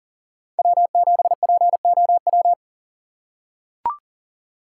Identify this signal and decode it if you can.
Morse code, heard as a keyed tone, about 40 wpm, 700 Hz. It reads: W7POW
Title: USA Call Signs - 5 Characters - 4x Character Spacing - 40wpm